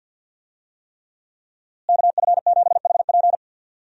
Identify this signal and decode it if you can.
Morse code, heard as a keyed tone, about 50 wpm, 700 Hz. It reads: KF7HP
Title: USA Call Signs - 5 Characters - Repeated 3x - 50wpm